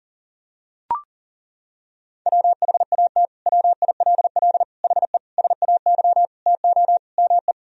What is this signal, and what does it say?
Morse code, heard as a keyed tone, about 40 wpm, 700 Hz. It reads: WHAT WILL HE SAY TO ME
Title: Sentences from Top 1000 Words - Review Part 3 - 40wpm